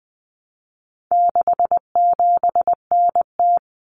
Morse code, heard as a keyed tone, about 20 wpm, 700 Hz. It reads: 67NT
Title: Common Cut Numbers - 4-Digit 20wpm